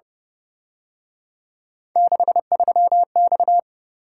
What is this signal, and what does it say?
Morse code, heard as a keyed tone, about 31 wpm, 700 Hz. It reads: 63X